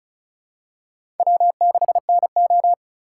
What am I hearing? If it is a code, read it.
Morse code, heard as a keyed tone, about 36 wpm, 700 Hz. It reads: W6NO
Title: USA Call Signs - 4 Characters - 2x Character Spacing - 35wpm